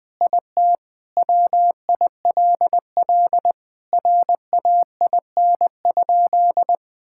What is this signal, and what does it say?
Morse code, heard as a keyed tone, about 20 wpm, 700 Hz. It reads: IT WILL RAIN?
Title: Sentences with 4-Word Phrases - 20wpm